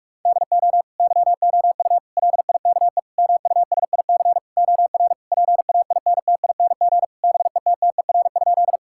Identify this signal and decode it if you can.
Morse code, heard as a keyed tone, about 46 wpm, 700 Hz. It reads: DO YOU LIKE MUSIC OR PAINTING BETTER?
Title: Sentences from Top 1000 Words - Review Part 6 - 45wpm